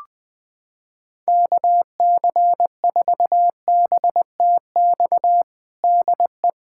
Morse code, heard as a keyed tone, about 20 wpm, 700 Hz. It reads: KC4BTX DE
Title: USA Callsign DE Callsign - 6 Characters - 20wpm